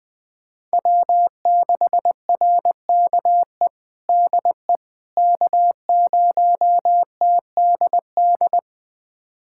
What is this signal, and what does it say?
Morse code, heard as a keyed tone, about 19 wpm, 700 Hz. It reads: W6RKE DE K0TDD
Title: USA Callsign DE Callsign - 5 Characters - 20wpm